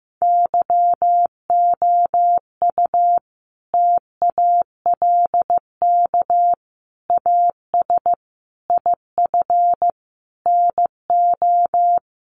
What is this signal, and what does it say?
Morse code, heard as a keyed tone, about 15 wpm, 700 Hz. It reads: YOU TALK AS IF NO
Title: Sentences from Top 1100 Words 15wpm